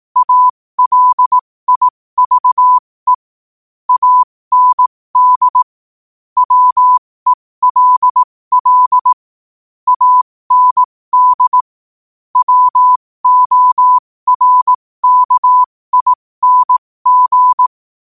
Morse code, heard as a keyed tone, about 18 wpm, spaced as for 15 wpm, 1000 Hz. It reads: ALIVE AND WELL AND WORKING
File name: qotdfiles / 15wpm